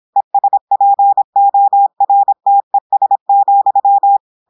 Morse code, heard as a keyed tone, about 26 wpm, 800 Hz. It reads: ESPORTES,
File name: portugais texte en clair 2 vitesse 26 mots minutes